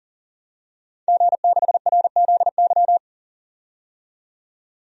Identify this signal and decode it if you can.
Morse code, heard as a keyed tone, about 40 wpm, 700 Hz. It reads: G6RZY
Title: DXCC-100 Call Signs - 5 Characters - 40wpm